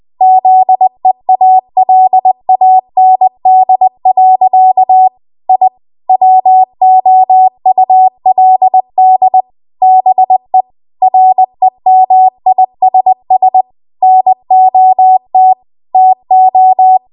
Morse code, heard as a keyed tone, about 20 wpm, 750 Hz. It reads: ZEALAND. I WOULD BE REMISS NOT TO
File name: Archive / 20 WPM / 20WPM